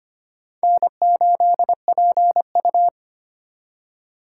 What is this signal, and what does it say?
Morse code, heard as a keyed tone, about 31 wpm, 700 Hz. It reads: N8PU